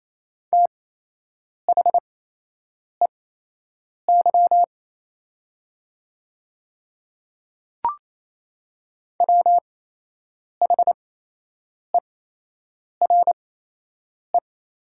Morse code, heard as a keyed tone, about 28 wpm, spaced as for 8 wpm, 700 Hz. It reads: THEY WHERE
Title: 8x Character Spacing - Top 200 Words - 28wpm